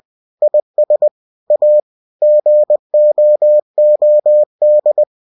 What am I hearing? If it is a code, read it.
Morse code, heard as a keyed tone, about 20 wpm, 600 Hz. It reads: IS A GOOD
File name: cw-books / winnie-the-pooh / winnie-20wpm-page-0053